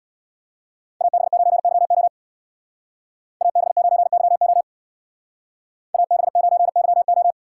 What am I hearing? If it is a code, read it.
Morse code, heard as a keyed tone, about 61 wpm, 700 Hz. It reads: AB9YO AB9YO AB9YO
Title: Speed Racing - USA Call Signs 50wpm